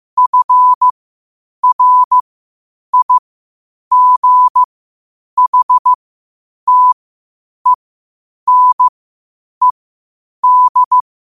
Morse code, heard as a keyed tone, about 15 wpm, spaced as for 8 wpm, 1000 Hz. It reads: FRIGHTENED
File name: qotdfiles / 8wpm